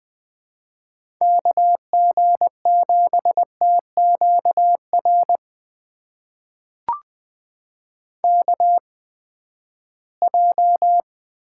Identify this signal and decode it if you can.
Morse code, heard as a keyed tone, about 21 wpm, spaced as for 14 wpm, 700 Hz. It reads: KG7TQR KJ
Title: USA Call Signs - 6 Characters - 8x Character Spacing - 20wpm